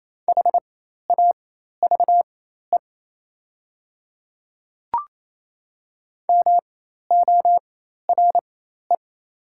Morse code, heard as a keyed tone, about 28 wpm, spaced as for 13 wpm, 700 Hz. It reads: HAVE MORE
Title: 4x Character Spacing - Top 100 Words - 28wpm